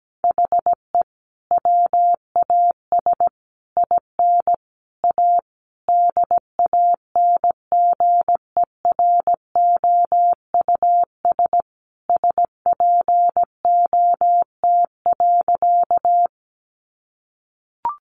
Morse code, heard as a keyed tone, about 17 wpm, 700 Hz. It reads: HE WAS IN A DANGEROUS SPOT.
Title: Sentences from Top 1200 Words 17wpm